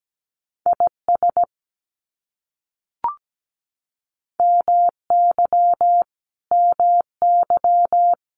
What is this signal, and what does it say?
Morse code, heard as a keyed tone, about 17 wpm, 700 Hz. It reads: IS MY MY
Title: Top 500 Words Repeated 17wpm